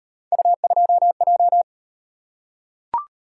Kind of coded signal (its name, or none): Morse code